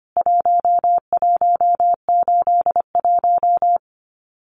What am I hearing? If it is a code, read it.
Morse code, heard as a keyed tone, about 25 wpm, 700 Hz. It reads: 1181
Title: CWT Exchange 25wpm